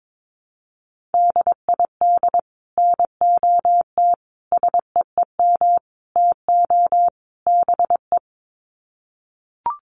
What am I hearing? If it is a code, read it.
Morse code, heard as a keyed tone, about 22 wpm, 700 Hz. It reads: DID NOT SEEM TO BE
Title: Sets of 5 Words Encore 22wpm